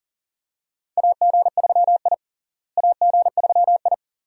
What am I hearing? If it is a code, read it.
Morse code, heard as a keyed tone, about 40 wpm, 700 Hz. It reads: AG3I AG3I
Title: USA Call Signs - 4 Characters - Repeated 2x - 40wpm